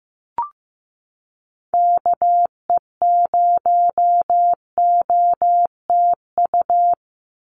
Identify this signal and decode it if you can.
Morse code, heard as a keyed tone, about 15 wpm, 700 Hz. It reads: KE0OTU
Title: USA Call Signs 15wpm